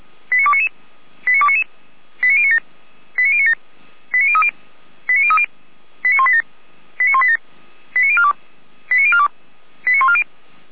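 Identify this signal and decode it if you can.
Five-tone selective call (ZVEI), heard as quick runs of five tones, tones of about 70 ms. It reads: 89299 89299 89097 89097 89030 89030 89187 89187 89042 89042 89140
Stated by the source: ZVEI five-tone call received with an SDR.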